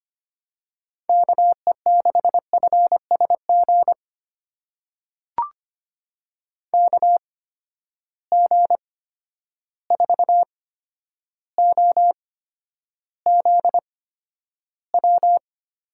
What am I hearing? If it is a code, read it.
Morse code, heard as a keyed tone, about 25 wpm, spaced as for 11 wpm, 700 Hz. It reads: KE6FSG KG4OZW
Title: USA Call Signs - 6 Characters - 8x Character Spacing - 25wpm